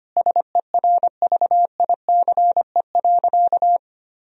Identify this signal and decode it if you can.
Morse code, heard as a keyed tone, about 25 wpm, 700 Hz. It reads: SERVICE.
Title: Sentences from Top 1000 Words - Review Part 4 - 25wpm